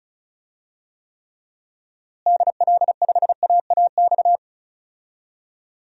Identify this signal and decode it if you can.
Morse code, heard as a keyed tone, about 36 wpm, 700 Hz. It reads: DL5AAX
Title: DXCC-100 Call Signs - 6 Characters - Repeated 2x - 35wpm